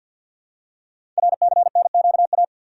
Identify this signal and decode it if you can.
Morse code, heard as a keyed tone, about 48 wpm, 700 Hz. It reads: RCNQA
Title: Code Groups - 5 Letters - 50wpm